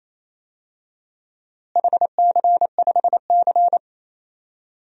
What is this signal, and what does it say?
Morse code, heard as a keyed tone, about 28 wpm, 700 Hz. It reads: HC5C
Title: DXCC-100 Call Signs - 4 Characters - Repeated 3x - 28wpm